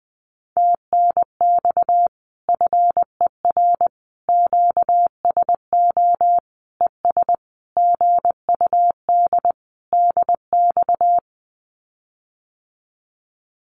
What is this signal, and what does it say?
Morse code, heard as a keyed tone, about 20 wpm, 700 Hz. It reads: TNX FER QSO ES GUD DX
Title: Common QSO Phrases Encore 20wpm